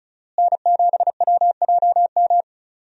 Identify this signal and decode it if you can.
Morse code, heard as a keyed tone, about 34 wpm, 700 Hz. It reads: N7WJM
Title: USA Call Signs - 5 Characters - Repeated 3x - 35wpm